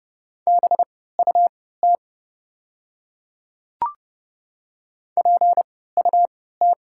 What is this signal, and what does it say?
Morse code, heard as a keyed tone, about 30 wpm, spaced as for 17 wpm, 690 Hz. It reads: BUT PUT
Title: 3x Character Spacing - Top 500 Words - 30wpm